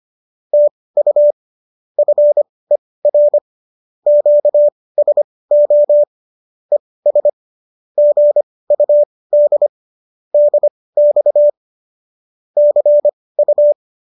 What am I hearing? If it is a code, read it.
Morse code, heard as a keyed tone, about 25 wpm, spaced as for 18 wpm, 590 Hz. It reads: TU FER QSO ES GUD DX CU